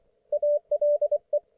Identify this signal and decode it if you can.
Morse code, heard as a keyed tone, about 24 wpm, 590 Hz. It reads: ALE
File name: Archiv / CUC212 / ok1fhi cul212